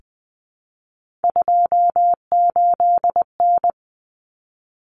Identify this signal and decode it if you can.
Morse code, heard as a keyed tone, about 20 wpm, 700 Hz. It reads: 28N
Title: Common Cut Numbers - 3-Digit 20wpm